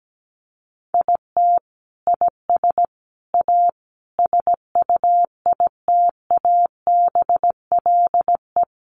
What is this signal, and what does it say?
Morse code, heard as a keyed tone, about 17 wpm, 700 Hz. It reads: IT IS A SUITABLE
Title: Sentences from Top 1400 Words 17wpm